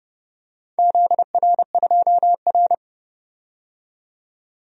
Morse code, heard as a keyed tone, about 29 wpm, 700 Hz. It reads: ZR2R
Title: DXCC-100 Call Signs - 4 Characters - 30wpm